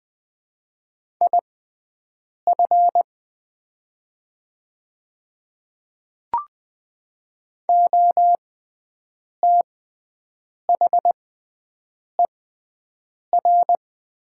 Morse code, heard as a keyed tone, about 20 wpm, spaced as for 7 wpm, 700 Hz. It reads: IF OTHER
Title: 6x Character Spacing - Top 100 Words - 20wpm